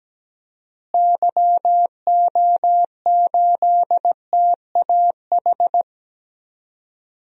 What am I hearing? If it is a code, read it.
Morse code, heard as a keyed tone, about 17 wpm, 700 Hz. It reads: YO8TAH
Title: DXCC-100 Call Signs - 6 Characters - Repeated 2x - 17wpm